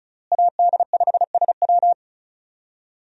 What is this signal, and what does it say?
Morse code, heard as a keyed tone, about 36 wpm, 700 Hz. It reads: AD5SW
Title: USA Call Signs - 5 Characters - Repeated 3x - 35wpm